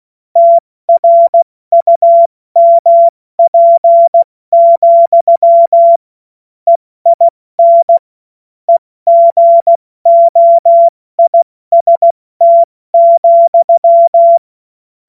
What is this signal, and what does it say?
Morse code, heard as a keyed tone, about 16 wpm, spaced as for 14 wpm, 680 Hz. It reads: TRUMP, EIN EGOIST,
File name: Morsekurs / Witze / Witz 16. Februar 2021, 70 F80